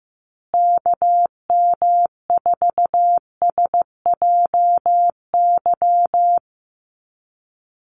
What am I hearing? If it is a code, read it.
Morse code, heard as a keyed tone, about 15 wpm, 700 Hz. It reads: KM4SJY
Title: USA Call Signs Repeated 2x 15wpm